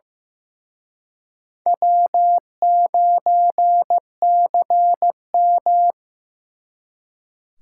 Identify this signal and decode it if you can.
Morse code, heard as a keyed tone, about 18 wpm, 700 Hz. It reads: W9CM
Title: Speed Racing - USA 1x2 Call Signs 15wpm